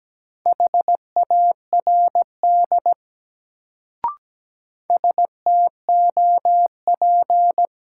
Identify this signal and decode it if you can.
Morse code, heard as a keyed tone, about 17 wpm, 700 Hz. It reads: HARD STOP
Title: Sentences from Top 300 Words - New Word Drill - 17wpm